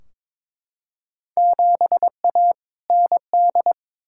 Morse code, heard as a keyed tone, about 22 wpm, 700 Hz. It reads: 7A ND
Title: Field Day Exchanges Encore - 22wpm